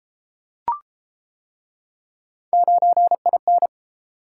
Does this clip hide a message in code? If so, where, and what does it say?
Yes: Morse code from 2.5 to 3.7 s, heard as a keyed tone, about 30 wpm, 700 Hz. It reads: 9IN